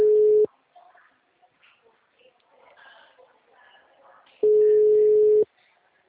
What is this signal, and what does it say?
Ringback tone (Europe, 425 Hz).